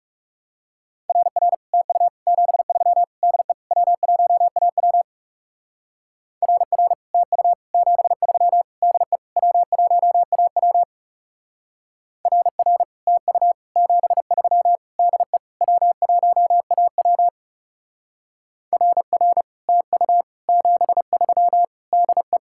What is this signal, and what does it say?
Morse code, heard as a keyed tone, about 36 wpm, 700 Hz. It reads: RR TU 73 DE W1AW RR TU 73 DE W1AW RR TU 73 DE W1AW RR TU 73 DE
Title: Speed-Racing Common QSO Phrases - 30wpm